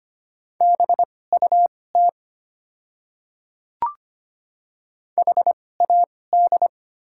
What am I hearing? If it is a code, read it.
Morse code, heard as a keyed tone, about 25 wpm, spaced as for 18 wpm, 690 Hz. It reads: BUT HAD
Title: 2x Character Spacing - Top 100 Words - 25wpm